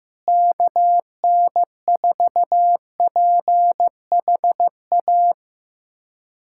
Morse code, heard as a keyed tone, about 15 wpm, 700 Hz. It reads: KN4PHA
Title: USA Callsign DE Callsign 15wpm